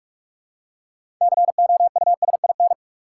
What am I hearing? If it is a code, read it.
Morse code, heard as a keyed tone, about 46 wpm, 700 Hz. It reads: COUSIN
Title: Six Letter Words - 45wpm